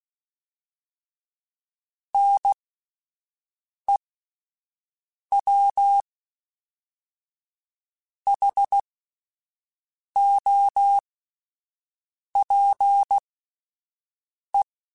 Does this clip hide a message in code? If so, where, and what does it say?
Yes: Morse code from 2.1 to 14.6 s, heard as a keyed tone, about 16 wpm, spaced as for 5 wpm, 800 Hz. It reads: NEWHOPE